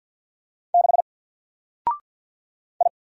Morse code, heard as a keyed tone, about 48 wpm, 700 Hz. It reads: 6 I
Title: Single Character (Letter:Number:Punctuation:Prosign) Alt 50wpm